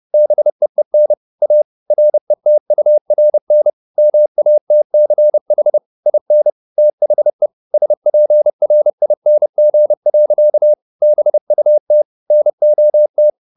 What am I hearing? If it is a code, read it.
Morse code, heard as a keyed tone, about 30 wpm, 600 Hz. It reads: BEEN A RETURN MATCH IN THE SPRING. BUT NOT
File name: the old man and the sea 2 30wpm